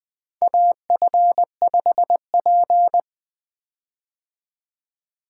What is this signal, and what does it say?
Morse code, heard as a keyed tone, about 20 wpm, 700 Hz. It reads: AF5P
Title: K1USN SST QSO Practice - 20wpm